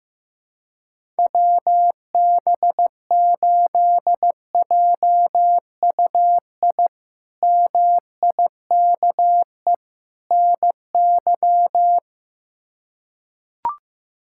Morse code, heard as a keyed tone, about 15 wpm, 700 Hz. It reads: WB8JUI MIKE NY
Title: K1USN SST QSO Practice - 15wpm